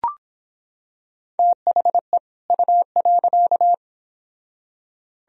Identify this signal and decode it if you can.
Morse code, heard as a keyed tone, about 24 wpm, 700 Hz. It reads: THE U.